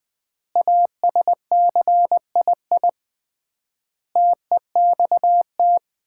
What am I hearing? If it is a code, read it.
Morse code, heard as a keyed tone, about 20 wpm, 700 Hz. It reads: ASCII TEXT
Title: 3x Word-Spacing - Sets of 5 Words - 20wpm